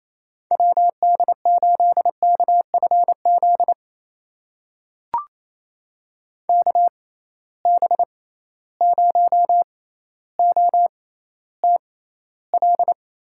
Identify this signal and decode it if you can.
Morse code, heard as a keyed tone, about 28 wpm, spaced as for 14 wpm, 700 Hz. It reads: WD8KFZ KB0OTL
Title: USA Call Signs - 6 Characters - 6x Character Spacing - 28wpm